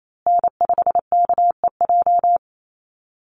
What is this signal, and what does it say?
Morse code, heard as a keyed tone, about 28 wpm, 700 Hz. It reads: N5KEJ
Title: USA Call Signs Repeated 2x 28wpm